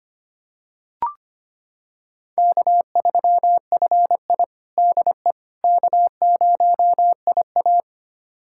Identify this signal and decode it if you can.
Morse code, heard as a keyed tone, about 25 wpm, 700 Hz. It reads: K3FI DE K0IA
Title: USA Callsign DE Callsign - 4 Characters - 25wpm